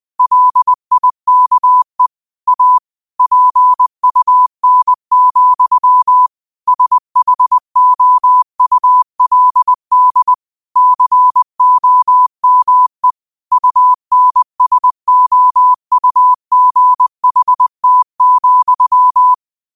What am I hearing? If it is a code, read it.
Morse code, heard as a keyed tone, about 20 wpm, 1000 Hz. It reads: LIKE A PUN, SHOULD COME UNSOUGHT,
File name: qotdfiles / 20wpm